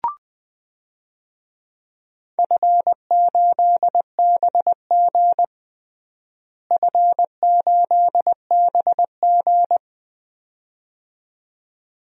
Morse code, heard as a keyed tone, about 20 wpm, 700 Hz. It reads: F8BG F8BG
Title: DXCC-100 Call Signs - 4 Characters - Repeated 2x - 20wpm